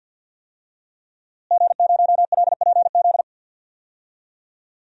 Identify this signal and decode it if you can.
Morse code, heard as a keyed tone, about 51 wpm, 700 Hz. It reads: G0LPZ
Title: DXCC-100 Call Signs - 5 Characters - Repeated 3x - 50wpm